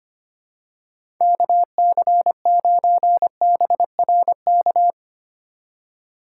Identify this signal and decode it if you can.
Morse code, heard as a keyed tone, about 25 wpm, 700 Hz. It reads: KC9BRK
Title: USA Call Signs - 4x Character Spacing - 25wpm